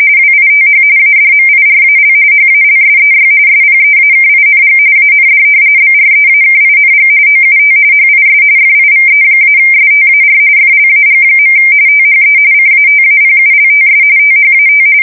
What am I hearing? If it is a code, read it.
RTTY, about 45.45 baud. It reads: WITH ADMIRAL HALSEY SPEAKING AND BREAKING OUT HIS FIVE-STAR FLAG. THE CATAPULTS THAT
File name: fsk170 / battleshipnj